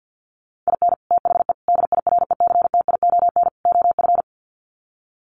Morse code, heard as a keyed tone, about 50 wpm, 700 Hz. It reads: IN THE DIRECTION OF
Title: Sets of 4 Words 50wpm